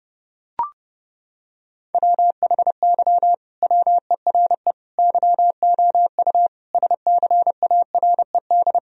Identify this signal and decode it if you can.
Morse code, heard as a keyed tone, about 30 wpm, 700 Hz. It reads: WHY WERE YOU SCARED
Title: Sentences from Top 1000 Words - Review Part 4 - 30wpm